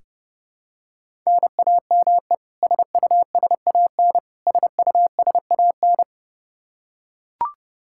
Morse code, heard as a keyed tone, about 30 wpm, 700 Hz. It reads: NAME SUSAN SUSAN
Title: Practice QSO - Exchange 1 - Beginning - 30wpm